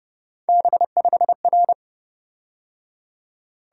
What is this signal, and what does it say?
Morse code, heard as a keyed tone, about 29 wpm, 700 Hz. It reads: B5R